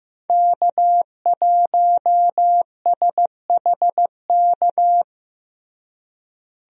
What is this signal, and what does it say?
Morse code, heard as a keyed tone, about 15 wpm, 700 Hz. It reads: K1SHK
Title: USA Call Signs - 5 Characters - Repeated 2x - 15wpm